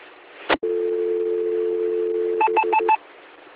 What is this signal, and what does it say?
Dial tone (North America), 350 + 440 Hz.
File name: recordings / AC 2017 Anthrocon / qsorec SimplexLogic EDT